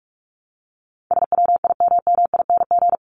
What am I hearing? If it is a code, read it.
Morse code, heard as a keyed tone, about 44 wpm, 700 Hz. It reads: SWIMMING
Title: Top 1501-1750 Words 45wpm